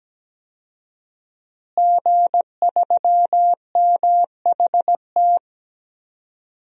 Morse code, heard as a keyed tone, about 17 wpm, 700 Hz. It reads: G3MHT
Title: DXCC-100 Call Signs - 5 Characters - Repeated 3x - 17wpm